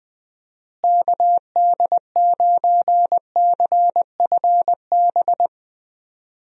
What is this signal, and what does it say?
Morse code, heard as a keyed tone, about 20 wpm, 700 Hz. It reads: KD9CFB
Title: USA Call Signs - 6 Characters - Repeated 2x - 20wpm